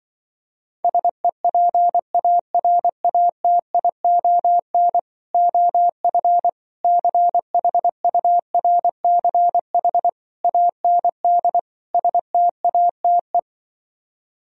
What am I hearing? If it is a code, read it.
Morse code, heard as a keyed tone, about 25 wpm, 700 Hz. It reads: SEPARATION OF CHURCH AND STATE
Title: Speed-Racing - Sets of 5 Words 20wpm